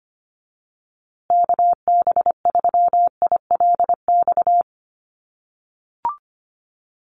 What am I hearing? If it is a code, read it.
Morse code, heard as a keyed tone, about 25 wpm, 700 Hz. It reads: KB3ILX
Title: USA Call Signs Repeated 3x 25wpm